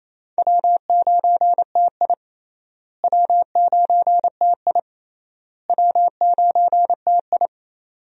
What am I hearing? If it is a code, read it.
Morse code, heard as a keyed tone, about 27 wpm, 700 Hz. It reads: W9TI W9TI W9TI
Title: USA Call Signs - 4 Characters - Repeated 3x - 28wpm